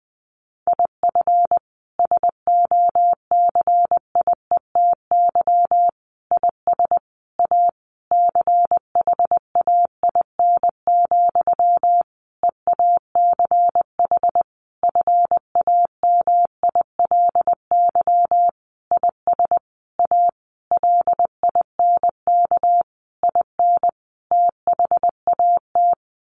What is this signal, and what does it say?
Morse code, heard as a keyed tone, about 20 wpm, 700 Hz. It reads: IF SOCIETY IS A CHAIN, EACH FAMILY IS A LINK IN THAT
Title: Sentences from Top 1500 Words 20wpm